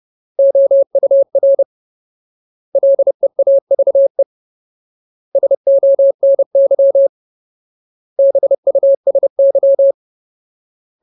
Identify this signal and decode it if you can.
Morse code, heard as a keyed tone, about 30 wpm, 550 Hz. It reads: OUR LEAVE SONY BUSY
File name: Ultimate Word List / 30WPM Ultimate